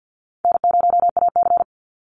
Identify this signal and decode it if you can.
Morse code, heard as a keyed tone, about 50 wpm, 700 Hz. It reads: N0AC
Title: Easy Call Signs Repeated 3x - 50wpm